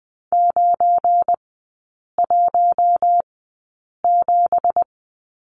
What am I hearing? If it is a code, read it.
Morse code, heard as a keyed tone, about 20 wpm, spaced as for 8 wpm, 700 Hz. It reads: 917
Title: 172 Lesson 28e - Number 9 - Three Characters 20wpm